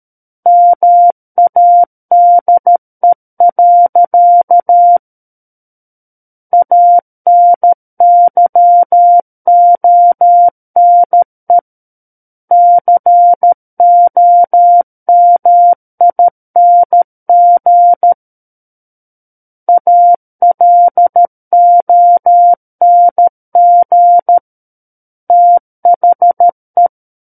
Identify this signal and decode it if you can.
Morse code, heard as a keyed tone, about 13 wpm, 700 Hz. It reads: MADE. ANYONE COMING ALONG THE
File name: Book One / 04-Chapter 4 - 13 WPM